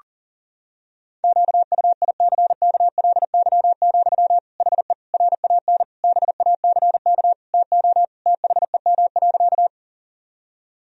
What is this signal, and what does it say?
Morse code, heard as a keyed tone, about 37 wpm, 700 Hz. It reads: QUICKLY, HE RAN BACK TO THEM.
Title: Sentences from Top 1000 Words - Review Part 3 - 40wpm